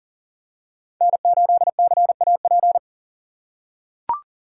Morse code, heard as a keyed tone, about 40 wpm, 700 Hz. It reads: N8CAP
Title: USA Call Signs - 5 Characters - 4x Character Spacing - 40wpm